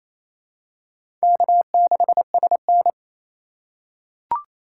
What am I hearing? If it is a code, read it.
Morse code, heard as a keyed tone, about 28 wpm, 700 Hz. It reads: K6SN
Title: USA Call Signs - 4 Characters - 3x Character Spacing - 28wpm